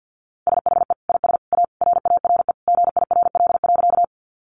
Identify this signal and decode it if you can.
Morse code, heard as a keyed tone, about 50 wpm, 700 Hz. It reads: SHE IS A RARE GIRL.
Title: Sentences from Top 1100 Words 50wpm